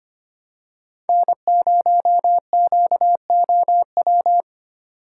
Morse code, heard as a keyed tone, about 25 wpm, 700 Hz. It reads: N0QOW
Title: USA Call Signs - 6x Character Spacing - 25wpm